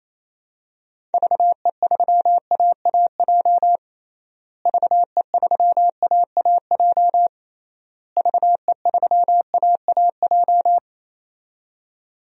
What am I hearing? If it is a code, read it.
Morse code, heard as a keyed tone, about 28 wpm, 700 Hz. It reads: VE3AAJ VE3AAJ VE3AAJ
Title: DXCC-100 Call Signs - 6 Characters - Repeated 3x - 28wpm